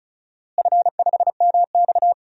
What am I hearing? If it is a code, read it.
Morse code, heard as a keyed tone, about 34 wpm, 700 Hz. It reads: F5MX
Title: DXCC-100 Call Signs - 4 Characters - Repeated 3x - 35wpm